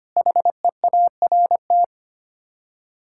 Morse code, heard as a keyed tone, about 25 wpm, 700 Hz. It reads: HEART
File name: Sets of 9 Words - 25wpm